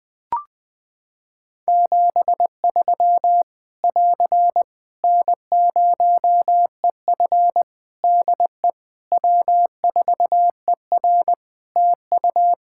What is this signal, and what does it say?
Morse code, heard as a keyed tone, about 20 wpm, 700 Hz. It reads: 73 <AR> N0EF DE W4ER TU
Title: Practice QSO - Exchange 4 - Ending - 20wpm